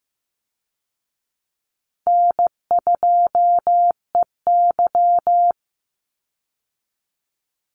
Morse code, heard as a keyed tone, about 15 wpm, 700 Hz. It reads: N2EY
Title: Call Signs Repeated 2x - 15wpm